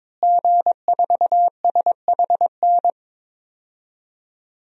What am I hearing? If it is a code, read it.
Morse code, heard as a keyed tone, about 22 wpm, 700 Hz. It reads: G4SHN
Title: DXCC-100 Call Signs - 5 Characters - Repeated 2x - 22wpm